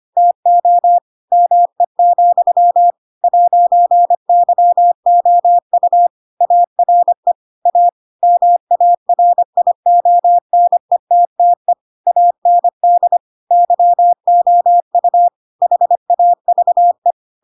Morse code, heard as a keyed tone, about 25 wpm, 700 Hz. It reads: TO ME, 'YOU ARE A MARIONETTE AND YOU HAVE
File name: Pinocchio / Pinocchio ch7 25 25 ik1hge